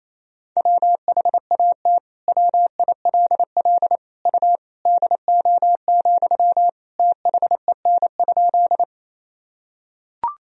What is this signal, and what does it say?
Morse code, heard as a keyed tone, about 28 wpm, 700 Hz. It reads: WHAT WILL U DO, THEN?
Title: Sentences from Top 100 Words and Abbreviations 28wpm